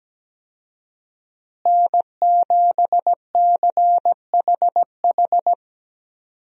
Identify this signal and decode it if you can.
Morse code, heard as a keyed tone, about 17 wpm, 700 Hz. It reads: N7CHH
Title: USA Call Signs - 5 Characters - Repeated 2x - 17wpm